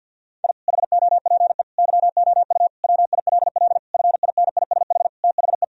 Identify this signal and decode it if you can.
Morse code, heard as a keyed tone, about 51 wpm, 700 Hz. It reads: I HOPE YOU WILL FINISH THE
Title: Sentences from Top 1000 Words - Review Part 3 - 50wpm